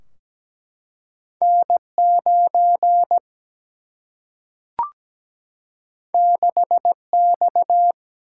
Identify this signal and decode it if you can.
Morse code, heard as a keyed tone, about 17 wpm, 700 Hz. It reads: N9 6X
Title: Single-Letter Single-Number 17wpm